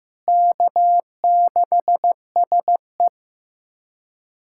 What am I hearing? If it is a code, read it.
Morse code, heard as a keyed tone, about 15 wpm, 700 Hz. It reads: K6SE
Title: USA Callsign DE Callsign - 4 Characters - 15wpm